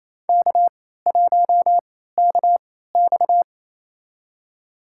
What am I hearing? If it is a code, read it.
Morse code, heard as a keyed tone, about 27 wpm, spaced as for 16 wpm, 700 Hz. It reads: K1KX
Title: USA Call Signs - 4 Characters - 3x Character Spacing - 28wpm